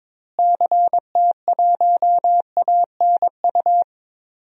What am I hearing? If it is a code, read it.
Morse code, heard as a keyed tone, about 21 wpm, 700 Hz. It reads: CT1ANU
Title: DXCC-100 Call Signs - 6 Characters - Repeated 3x - 22wpm